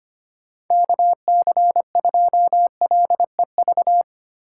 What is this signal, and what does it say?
Morse code, heard as a keyed tone, about 25 wpm, 700 Hz. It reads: KC2LEV
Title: USA Call Signs - 6 Characters - 6x Character Spacing - 25wpm